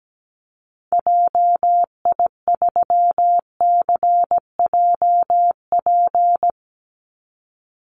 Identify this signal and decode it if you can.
Morse code, heard as a keyed tone, about 17 wpm, 700 Hz. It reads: JI3CJP
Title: Call Signs Repeated 2x - 17wpm